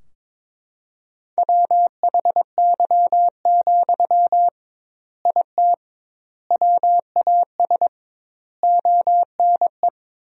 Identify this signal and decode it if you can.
Morse code, heard as a keyed tone, about 22 wpm, 700 Hz. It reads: WHY, IT WAS ONE
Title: 2x Word-Spacing - Sentences from Top 300 Words - 22wpm